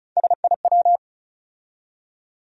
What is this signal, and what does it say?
Morse code, heard as a keyed tone, about 36 wpm, 700 Hz. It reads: SIW
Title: USA Call Sign Postfixes - 3 Letters 35wpm